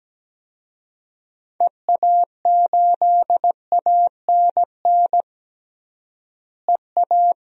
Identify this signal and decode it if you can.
Morse code, heard as a keyed tone, about 17 wpm, 700 Hz. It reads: EA8ANN EA
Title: DXCC-100 Call Signs - 6 Characters - Repeated 2x - 17wpm